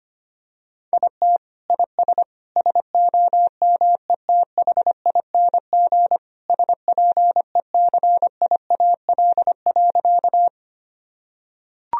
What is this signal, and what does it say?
Morse code, heard as a keyed tone, about 25 wpm, 700 Hz. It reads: IT IS SOMETHING SPECIAL.
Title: Sentences from Top 1000 Words - Review Part 3 - 25wpm